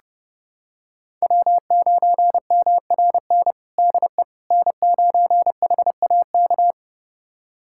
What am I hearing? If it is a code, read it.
Morse code, heard as a keyed tone, about 31 wpm, 700 Hz. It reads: W9MRN DE N9HAK
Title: USA Callsign DE Callsign - 5 Characters - 30wpm